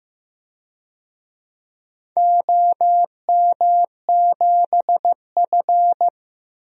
Morse code, heard as a keyed tone, about 15 wpm, 700 Hz. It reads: OM7F
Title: DXCC-100 Call Signs - 4 Characters - Repeated 2x - 15wpm